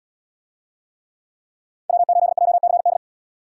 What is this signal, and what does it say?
Morse code, heard as a keyed tone, about 77 wpm, 700 Hz. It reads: W8JCD